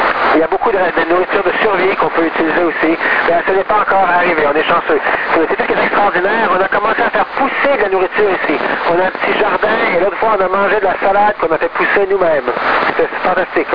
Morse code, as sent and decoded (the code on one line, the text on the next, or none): none